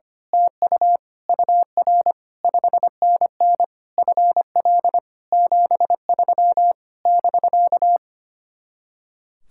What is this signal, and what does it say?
Morse code, heard as a keyed tone, about 24 wpm, 700 Hz. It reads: TU UR 5NN FL 73 <BK>
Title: Speed-Racing Common QSO Phrases - 25wpm